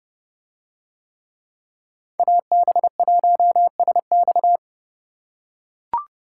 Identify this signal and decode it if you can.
Morse code, heard as a keyed tone, about 29 wpm, 700 Hz. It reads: AB1SX